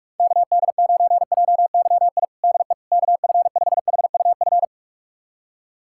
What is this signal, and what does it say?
Morse code, heard as a keyed tone, about 41 wpm, 700 Hz. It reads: KD9JYI DE KF5HUF
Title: USA Callsign DE Callsign 45wpm